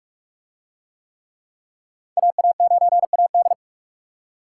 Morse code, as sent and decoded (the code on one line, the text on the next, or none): .- .- ----. .- -..
AA9AD